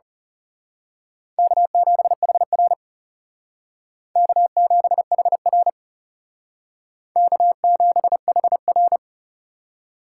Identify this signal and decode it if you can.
Morse code, heard as a keyed tone, about 34 wpm, 700 Hz. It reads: K7HR K7HR K7HR
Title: Speed Racing - USA 1x2 Call Signs 30wpm